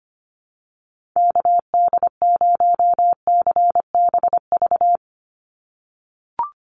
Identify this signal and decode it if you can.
Morse code, heard as a keyed tone, about 25 wpm, 700 Hz. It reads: KD0CBV
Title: USA Call Signs 25wpm